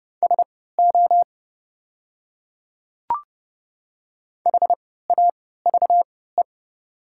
Morse code, heard as a keyed tone, about 30 wpm, spaced as for 17 wpm, 700 Hz. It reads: SO HAVE
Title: 3x Character Spacing - Top 100 Words - 30wpm